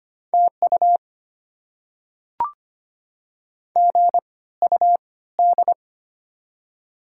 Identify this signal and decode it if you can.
Morse code, heard as a keyed tone, about 25 wpm, spaced as for 16 wpm, 700 Hz. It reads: TU GUD
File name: Top 20 QSO Elements - 3x Character Spacing - 25wpm